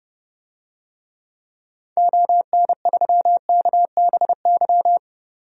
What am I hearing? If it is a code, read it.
Morse code, heard as a keyed tone, about 30 wpm, 700 Hz. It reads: ON3KBY
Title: DXCC-100 Call Signs - 6 Characters - 30wpm